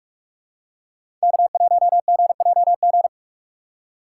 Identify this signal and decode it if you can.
Morse code, heard as a keyed tone, about 47 wpm, 690 Hz. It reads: K1GJG
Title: USA Call Signs - 5 Characters - 2x Character Spacing - 45wpm